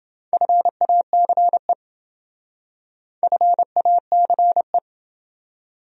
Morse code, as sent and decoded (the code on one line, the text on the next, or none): ..-. .- -.-. / ..-. .- -.-. .
FACE FACE